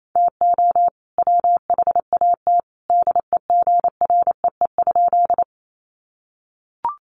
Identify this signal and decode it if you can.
Morse code, heard as a keyed tone, about 28 wpm, 700 Hz. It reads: TO WHAT DEGREE?
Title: Sentences from Top 1500 Words 28wpm